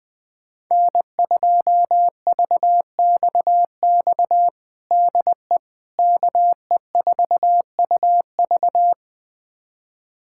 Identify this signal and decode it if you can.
Morse code, heard as a keyed tone, about 20 wpm, 700 Hz. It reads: N2VXX DE KE4UV
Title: USA Callsign DE Callsign - 5 Characters - 20wpm